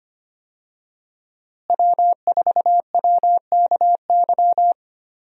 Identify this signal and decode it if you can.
Morse code, heard as a keyed tone, about 25 wpm, 700 Hz. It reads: W4WKY
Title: USA Call Signs - 5 Characters - Repeated 2x - 25wpm